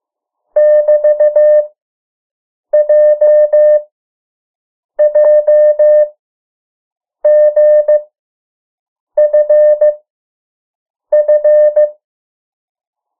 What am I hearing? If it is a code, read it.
Morse code, heard as a keyed tone, about 15 wpm, 600 Hz. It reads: <BT> J J G F F
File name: morse mp3 / 03 urok 2